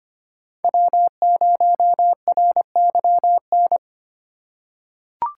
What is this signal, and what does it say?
Morse code, heard as a keyed tone, about 25 wpm, 700 Hz. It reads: W0RYN
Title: USA Call Signs - 5 Characters - 2x Character Spacing - 25wpm